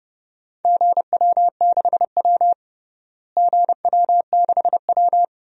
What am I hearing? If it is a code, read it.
Morse code, heard as a keyed tone, about 32 wpm, 700 Hz. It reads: GW6W GW6W
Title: DXCC-100 Call Signs - 4 Characters - Repeated 2x - 30wpm